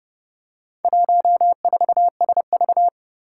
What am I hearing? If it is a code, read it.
Morse code, heard as a keyed tone, about 30 wpm, 700 Hz. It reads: 14SV
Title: Four Challenging Letter-Number Combinations 30wpm